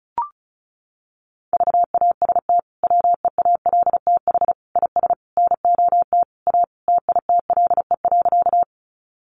Morse code, heard as a keyed tone, about 34 wpm, 700 Hz. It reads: VAST WEALTH IS NOT A TITLE.
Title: Sentences from Top 1300 Words 35wpm